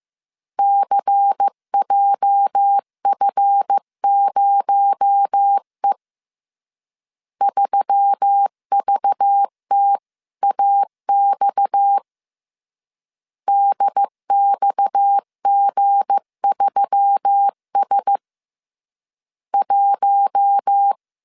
Morse code, heard as a keyed tone, about 15 wpm, 800 Hz. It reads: CJF0E 3VTAX DXG3S 1
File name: CW / DET / 1 9 1500C S C 14wpm 3245